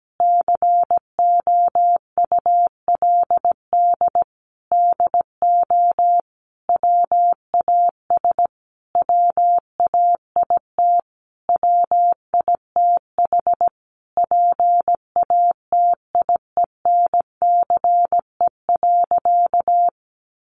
Morse code, heard as a keyed tone, about 17 wpm, 700 Hz. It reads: COULD DO WAS WAIT WITH PATIENCE.
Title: Sentences from Top 1300 Words 17wpm